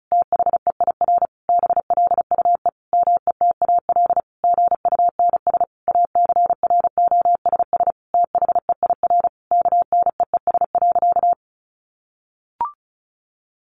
Morse code, heard as a keyed tone, about 34 wpm, 700 Hz. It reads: THEIR BLUE METAL GUNS ACROSS THEIR KNEES.
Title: A Sound Of Thunder 35wpm